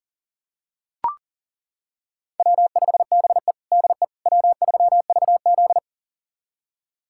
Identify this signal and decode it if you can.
Morse code, heard as a keyed tone, about 40 wpm, 700 Hz. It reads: W5BE DE W3VZ
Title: USA Callsign DE Callsign - 4 Characters - 40wpm